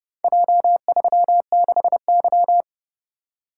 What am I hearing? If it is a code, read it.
Morse code, heard as a keyed tone, about 30 wpm, 700 Hz. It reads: J36Y